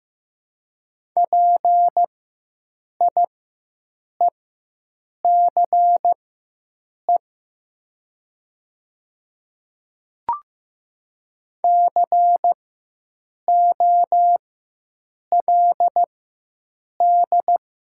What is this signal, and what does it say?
Morse code, heard as a keyed tone, about 15 wpm, spaced as for 7 wpm, 690 Hz. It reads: PIECE COLD
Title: 4x Character Spacing - Top 500 Words - 15wpm